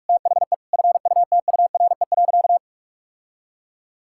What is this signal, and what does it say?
Morse code, heard as a keyed tone, about 46 wpm, 700 Hz. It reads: THE FUTURE.
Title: Sentences from Top 1000 Words - Review Part 5 - 45wpm